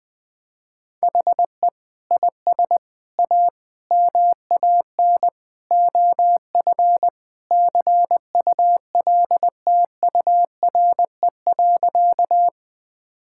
Morse code, heard as a keyed tone, about 20 wpm, 700 Hz. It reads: HE IS A MAN OF CULTURE.
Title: Sentences from Top 1000 Words - Review Part 6 - 20wpm